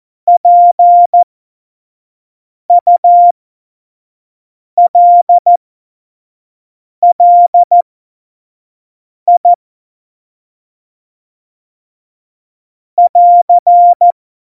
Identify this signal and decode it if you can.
Morse code, heard as a keyed tone, about 14 wpm, spaced as for 5 wpm, 700 Hz. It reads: PULLI <AR>
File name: Morsekurs / Witze / Witz 20. August 2019, 25 F70